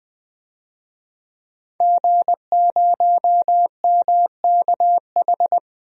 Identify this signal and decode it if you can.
Morse code, heard as a keyed tone, about 21 wpm, 700 Hz. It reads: G0MKH
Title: DXCC-100 Call Signs - 5 Characters - Repeated 2x - 20wpm